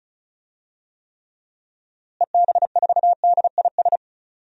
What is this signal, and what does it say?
Morse code, heard as a keyed tone, about 34 wpm, 700 Hz. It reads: EB4DIS
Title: DXCC-100 Call Signs - 6 Characters - Repeated 3x - 35wpm